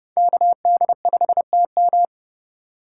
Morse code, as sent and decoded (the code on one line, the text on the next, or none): -.- -.. ..... - --
KD5TM